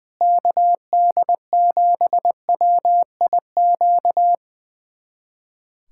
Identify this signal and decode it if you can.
Morse code, heard as a keyed tone, about 20 wpm, 700 Hz. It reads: KD7WIQ
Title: USA Callsign DE Callsign - 6 Characters - 20wpm